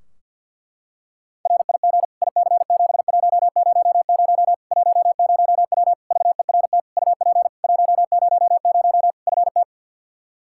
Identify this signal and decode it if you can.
Morse code, heard as a keyed tone, about 47 wpm, 700 Hz. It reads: RIG IC7100 10W VERT UP 100 FT
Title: Practice QSO - Exchange 2 - Info - 50wpm